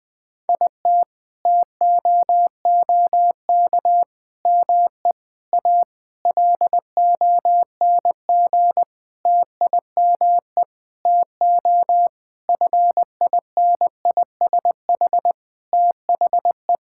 Morse code, heard as a keyed tone, about 20 wpm, 700 Hz. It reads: IT TOOK ME A LONG TIME TO FINISH THE
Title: Sentences with 7-Word Phrases - Spoken Beginning - 20wpm